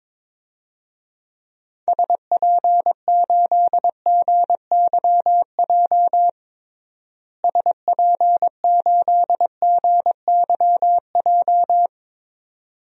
Morse code, heard as a keyed tone, about 22 wpm, 700 Hz. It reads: SP8GYJ SP8GYJ
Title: DXCC-100 Call Signs - 6 Characters - Repeated 2x - 22wpm